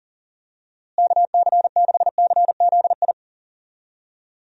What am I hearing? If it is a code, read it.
Morse code, heard as a keyed tone, about 38 wpm, 700 Hz. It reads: KC6CZI
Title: USA Call Signs - 6 Characters - Repeated 2x - 40wpm